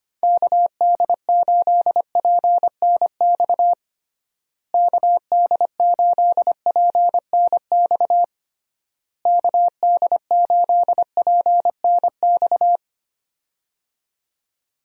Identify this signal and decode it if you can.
Morse code, heard as a keyed tone, about 25 wpm, 700 Hz. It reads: KD8PNX KD8PNX KD8PNX
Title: USA Call Signs - 6 Characters - Repeated 3x - 25wpm